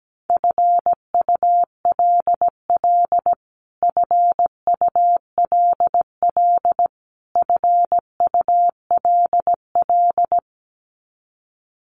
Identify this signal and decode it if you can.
Morse code, heard as a keyed tone, about 17 wpm, 700 Hz. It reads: FULL FULL FULL
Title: Top 500 Words Repeated 17wpm